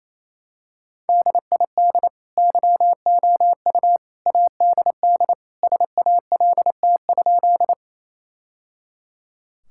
Morse code, heard as a keyed tone, about 28 wpm, 700 Hz. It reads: DID YOU ADD SALT?
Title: Sentences from Top 1000 Words - Review Part 5 - 28wpm